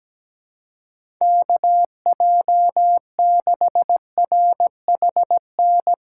Morse code, heard as a keyed tone, about 17 wpm, 700 Hz. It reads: KJ6RHN
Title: USA Callsign DE Callsign 17wpm